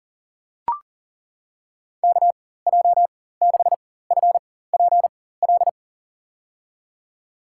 Morse code, heard as a keyed tone, about 40 wpm, spaced as for 19 wpm, 700 Hz. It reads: KJ6FPL
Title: USA Call Signs - 6 Characters - 4x Character Spacing - 40wpm